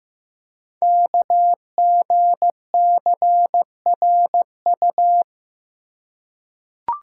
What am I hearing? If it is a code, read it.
Morse code, heard as a keyed tone, about 15 wpm, 700 Hz. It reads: KGCRU
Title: Code Groups - 5 Letters - 15wpm